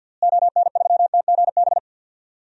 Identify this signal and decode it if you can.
Morse code, heard as a keyed tone, about 52 wpm, 700 Hz. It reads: ON3TGB
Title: DXCC-100 Call Signs - 6 Characters - Repeated 2x - 50wpm